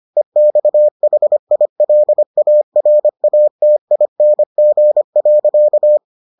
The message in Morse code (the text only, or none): EXHILARATING.